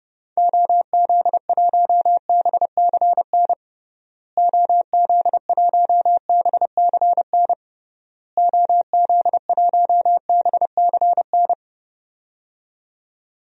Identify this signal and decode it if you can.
Morse code, heard as a keyed tone, about 30 wpm, 700 Hz. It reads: OZ1BCN OZ1BCN OZ1BCN
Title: DXCC-100 Call Signs - 6 Characters - Repeated 3x - 30wpm